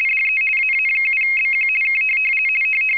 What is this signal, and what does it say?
RTTY, about 50 baud. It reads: DE AA7BQ K
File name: rtty425 / rtty50.wav / rtty50